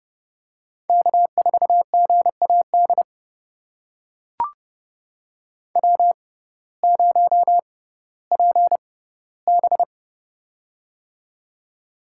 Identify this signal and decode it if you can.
Morse code, heard as a keyed tone, about 30 wpm, spaced as for 16 wpm, 700 Hz. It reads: K4GAD W0PB
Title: USA Call Signs - 6x Character Spacing - 30wpm